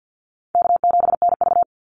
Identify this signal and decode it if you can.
Morse code, heard as a keyed tone, about 50 wpm, 700 Hz. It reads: K7NV
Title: Call Signs - 50wpm